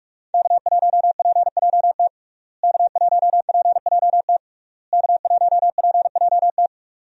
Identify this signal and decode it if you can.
Morse code, heard as a keyed tone, about 41 wpm, 700 Hz. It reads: K1PJT K1PJT K1PJT
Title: USA Call Signs - 5 Characters - Repeated 3x - 45wpm